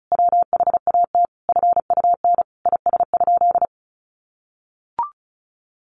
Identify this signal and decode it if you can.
Morse code, heard as a keyed tone, about 34 wpm, 700 Hz. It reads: WHAT FUN IS?
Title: Sentences from Top 700 Words 35wpm